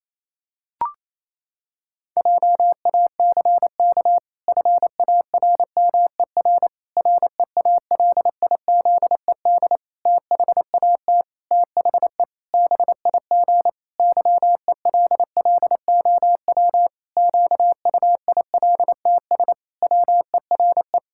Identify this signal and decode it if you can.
Morse code, heard as a keyed tone, about 27 wpm, 700 Hz. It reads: JACK FARMER REALIZED THAT THE BIG YELLOW QUILTS WERE
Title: Pangram Challenge v2 - 28wpm